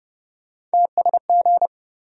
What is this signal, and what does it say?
Morse code, heard as a keyed tone, about 30 wpm, 700 Hz. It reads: TSG
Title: USA Call Sign Postfixes - 3 Letters 30wpm